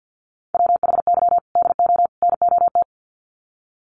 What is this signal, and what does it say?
Morse code, heard as a keyed tone, about 50 wpm, 700 Hz. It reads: WHY DO NOT
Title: Sets of 3 Words 50wpm